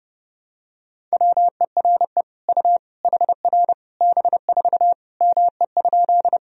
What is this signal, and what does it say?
Morse code, heard as a keyed tone, about 30 wpm, 700 Hz. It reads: WERE U HR B4 ME?
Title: Sentences from Top 200 Words and Abbreviations 30wpm